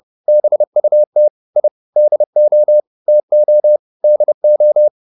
Morse code, heard as a keyed tone, about 30 wpm, 600 Hz. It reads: BUT I DO TO DO
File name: WD201 30